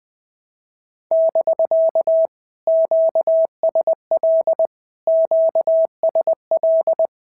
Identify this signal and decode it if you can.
Morse code, heard as a keyed tone, about 20 wpm, 650 Hz. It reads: <BK> QSL QSL
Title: POTA QSO Practice - 20wpm